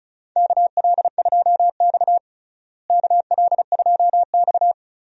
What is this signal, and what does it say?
Morse code, heard as a keyed tone, about 36 wpm, 700 Hz. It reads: KL2X KL2X
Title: USA Call Signs - 4 Characters - Repeated 3x - 35wpm